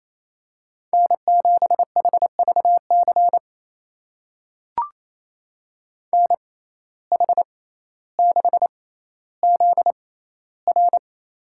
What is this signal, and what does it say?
Morse code, heard as a keyed tone, about 28 wpm, spaced as for 14 wpm, 700 Hz. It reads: N7HVC NH6ZR
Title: USA Call Signs - 5 Characters - 6x Character Spacing - 28wpm